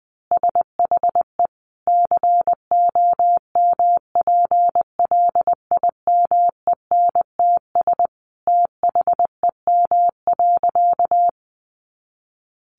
Morse code, heard as a keyed tone, about 20 wpm, 700 Hz. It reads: SHE COMPLIMENTS THEM.
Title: Easy Phrases 20wpm